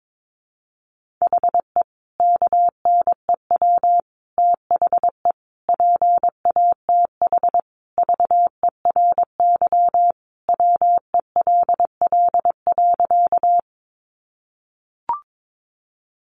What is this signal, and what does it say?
Morse code, heard as a keyed tone, about 22 wpm, 700 Hz. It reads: HE KNEW THE PATH VERY WELL.
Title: Sentences from Top 1400 Words 22wpm